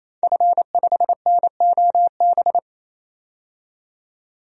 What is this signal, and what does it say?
Morse code, heard as a keyed tone, about 28 wpm, 700 Hz. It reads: F5NOB
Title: DXCC-100 Call Signs - 5 Characters - 28wpm